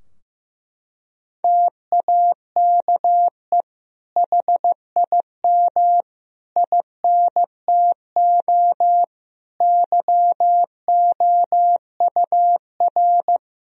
Morse code, heard as a keyed tone, about 16 wpm, 700 Hz. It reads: TAKE HIM INTO YOUR
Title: Sentences from Top 1000 Words - Review Part 4 - 15wpm